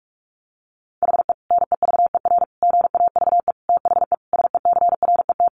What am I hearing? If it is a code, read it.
Morse code, heard as a keyed tone, about 44 wpm, 700 Hz. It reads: HE NEVER GAVE THE SECRET
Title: Sentences from Top 1200 Words 45wpm